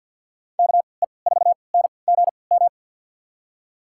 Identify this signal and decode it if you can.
Morse code, heard as a keyed tone, about 47 wpm, spaced as for 28 wpm, 700 Hz. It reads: KE4NGM
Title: USA Call Signs - 3x Character Spacing - 50wpm